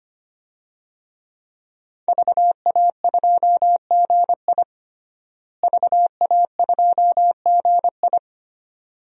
Morse code, heard as a keyed tone, about 25 wpm, 700 Hz. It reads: VA2GI VA2GI
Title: DXCC-100 Call Signs - 5 Characters - Repeated 2x - 25wpm